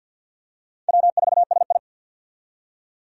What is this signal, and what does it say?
Morse code, heard as a keyed tone, about 52 wpm, 700 Hz. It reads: W4SI
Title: USA Call Signs - 4 Characters - 50wpm